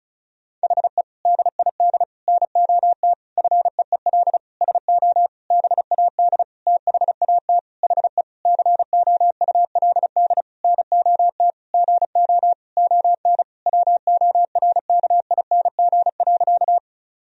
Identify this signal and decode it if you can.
Morse code, heard as a keyed tone, about 34 wpm, 700 Hz. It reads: HE DID NOT FEEL SO BAD THAT HE COULD NOT GO ON WORKING.
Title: Sentences from Top 1000 Words - Review Part 4 - 35wpm